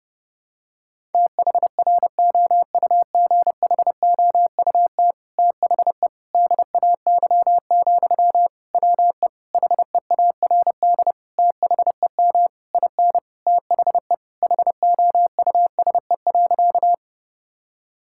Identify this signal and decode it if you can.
Morse code, heard as a keyed tone, about 30 wpm, 700 Hz. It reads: THROUGHOUT THE DAY, WE HEARD THEM IN THE HOUSE.
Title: Sentences from Top 1000 Words - Review Part 5 - 30wpm